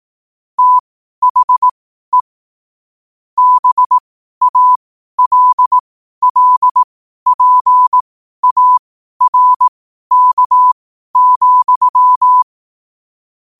Morse code, heard as a keyed tone, about 18 wpm, spaced as for 12 wpm, 1000 Hz. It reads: THE BALLPARK,
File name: qotdfiles / 12wpm